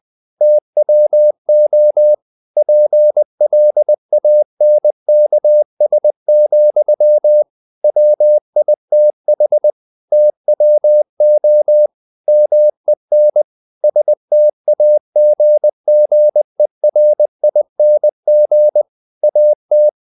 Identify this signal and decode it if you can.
Morse code, heard as a keyed tone, about 20 wpm, 600 Hz. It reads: TWO PLANKS, WITH TWO MEN STAGGERING AT
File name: the old man and the sea / the old man and the sea part 1 20wpm